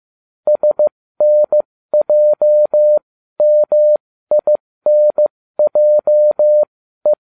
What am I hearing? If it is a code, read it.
Morse code, heard as a keyed tone, about 15 wpm, spaced as for 13 wpm, 600 Hz. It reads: SNJMINJE